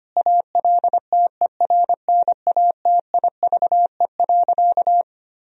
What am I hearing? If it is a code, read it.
Morse code, heard as a keyed tone, about 25 wpm, 700 Hz. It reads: ALTERNATIVE.
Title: Sentences from Top 1000 Words - Review Part 4 - 25wpm